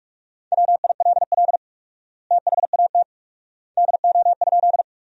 Morse code, heard as a keyed tone, about 44 wpm, 700 Hz. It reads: WILL THAT DO?
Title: 4x Word-Spacing - Sentences from Top 100 Words - 45wpm